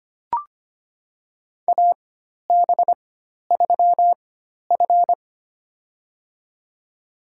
Morse code, heard as a keyed tone, about 25 wpm, spaced as for 12 wpm, 700 Hz. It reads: AB3F
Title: USA Call Signs - 4 Characters - 4x Character Spacing - 25wpm